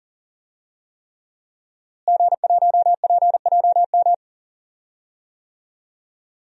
Morse code, heard as a keyed tone, about 37 wpm, 700 Hz. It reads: G1PJM
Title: DXCC-100 Call Signs - 5 Characters - Repeated 3x - 40wpm